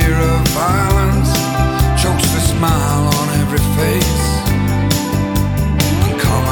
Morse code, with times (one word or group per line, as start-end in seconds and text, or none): none